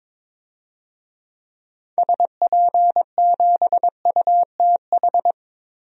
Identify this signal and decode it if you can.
Morse code, heard as a keyed tone, about 22 wpm, 700 Hz. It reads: SP7UTH
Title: DXCC-100 Call Signs - 6 Characters - Repeated 2x - 22wpm